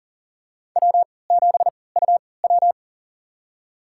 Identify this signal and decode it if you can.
Morse code, heard as a keyed tone, about 40 wpm, spaced as for 23 wpm, 700 Hz. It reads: W7UW
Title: USA Call Signs - 4 Characters - 3x Character Spacing - 40wpm